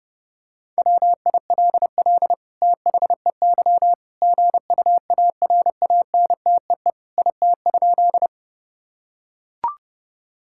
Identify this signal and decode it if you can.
Morse code, heard as a keyed tone, about 30 wpm, 700 Hz. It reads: WILL THEY GUARANTEE IT?
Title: Sentences from Top 1600 Words 30wpm